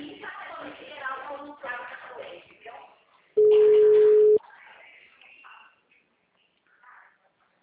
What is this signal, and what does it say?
Ringback tone (Europe, 425 Hz), with speech.